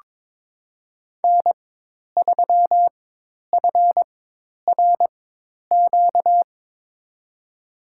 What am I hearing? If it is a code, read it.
Morse code, heard as a keyed tone, about 23 wpm, spaced as for 10 wpm, 700 Hz. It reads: N3FRQ
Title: USA Call Signs - 4x Character Spacing - 22wpm